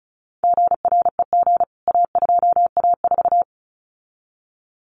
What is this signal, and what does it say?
Morse code, heard as a keyed tone, about 34 wpm, 700 Hz. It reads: GREG A2A4
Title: CWT Exchange 35wpm